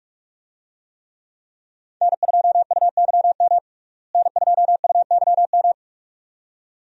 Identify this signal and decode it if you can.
Morse code, heard as a keyed tone, about 44 wpm, 700 Hz. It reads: N2UYM N2UYM
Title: USA Call Signs - 5 Characters - Repeated 2x - 45wpm